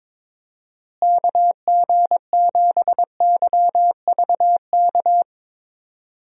Morse code, heard as a keyed tone, about 21 wpm, 700 Hz. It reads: KG7YVK
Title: USA Call Signs - 6 Characters - Repeated 2x - 22wpm